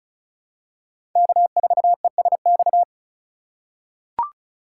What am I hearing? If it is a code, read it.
Morse code, heard as a keyed tone, about 34 wpm, 700 Hz. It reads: K4ESX
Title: USA Call Signs - 5 Characters - 4x Character Spacing - 35wpm